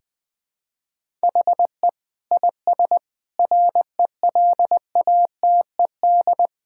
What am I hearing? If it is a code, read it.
Morse code, heard as a keyed tone, about 19 wpm, 700 Hz. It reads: HE IS RELATED
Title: Sentences from Top 1000 Words - Review Part 5 - 20wpm